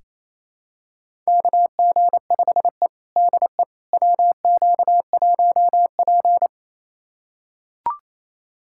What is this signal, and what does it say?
Morse code, heard as a keyed tone, about 28 wpm, 700 Hz. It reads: KG5E DE WQ1P
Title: USA Callsign DE Callsign - 4 Characters - 28wpm